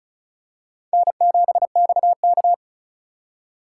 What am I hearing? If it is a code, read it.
Morse code, heard as a keyed tone, about 34 wpm, 700 Hz. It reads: N7XK
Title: USA Call Signs - 4 Characters - 4x Character Spacing - 35wpm